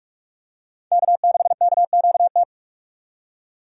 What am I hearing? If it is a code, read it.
Morse code, heard as a keyed tone, about 42 wpm, 700 Hz. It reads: K6KQT
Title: USA Call Signs - 5 Characters - 45wpm